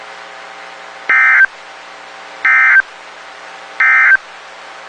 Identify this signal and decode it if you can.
Emergency Alert System (SAME) tones, heard as harsh buzzing bursts. It is heard from 1.0 to 4.3 s.
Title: WXK91 NOAA Weather Radio